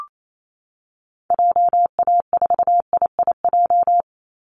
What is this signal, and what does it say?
Morse code, heard as a keyed tone, about 28 wpm, 700 Hz. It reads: JA4IIJ
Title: Call Signs Repeated 2x - 28wpm